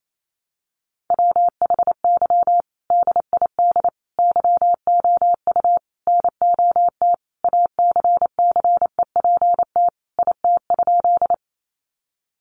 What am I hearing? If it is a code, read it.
Morse code, heard as a keyed tone, about 28 wpm, 700 Hz. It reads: WHY DID YOU NOT ACCEPT IT?
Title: Sentences from Top 1300 Words 28wpm